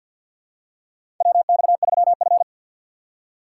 Morse code, heard as a keyed tone, about 52 wpm, 700 Hz. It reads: WX3F
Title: USA Call Signs - 4 Characters - 2x Character Spacing - 50wpm